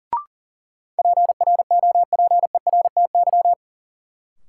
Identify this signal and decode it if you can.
Morse code, heard as a keyed tone, about 40 wpm, 700 Hz. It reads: PROPERTY
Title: Sentences from Top 1000 Words - New Word Drill - 40wpm